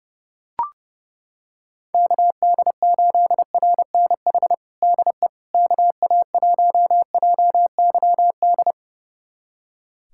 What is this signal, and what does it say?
Morse code, heard as a keyed tone, about 30 wpm, 700 Hz. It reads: KD8RNH DE KA1JYD
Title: USA Callsign DE Callsign 30wpm